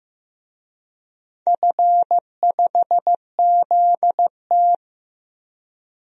Morse code, heard as a keyed tone, about 15 wpm, 700 Hz. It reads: F5ZT
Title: DXCC-100 Call Signs - 4 Characters - 15wpm